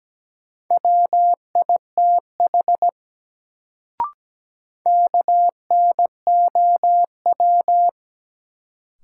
Morse code, heard as a keyed tone, about 17 wpm, 700 Hz. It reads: WITH KNOW
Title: Sets of 2 Words - New Word Drill - 17wpm